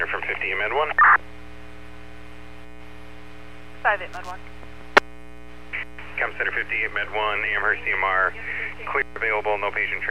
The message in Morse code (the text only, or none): none